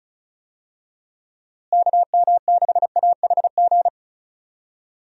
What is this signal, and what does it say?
Morse code, heard as a keyed tone, about 34 wpm, 700 Hz. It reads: KM6AHG
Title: USA Call Signs - 6 Characters - Repeated 2x - 35wpm